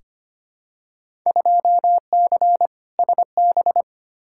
Morse code, heard as a keyed tone, about 25 wpm, 700 Hz. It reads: 2C SB
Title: Field Day Exchanges Encore - 25wpm